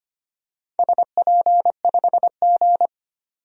Morse code, heard as a keyed tone, about 25 wpm, 700 Hz. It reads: SP5G
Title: DXCC-100 Call Signs - 4 Characters - 25wpm